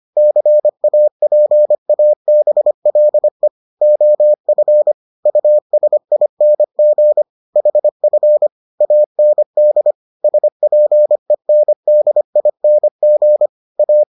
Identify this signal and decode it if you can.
Morse code, heard as a keyed tone, about 25 wpm, 600 Hz. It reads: CAPABLE OF USING HF AND SPENDING A